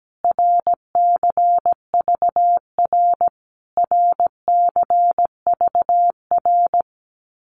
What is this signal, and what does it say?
Morse code, heard as a keyed tone, about 17 wpm, 700 Hz. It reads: RCVR RCVR
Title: Abbreviations Repeated 3x - 17wpm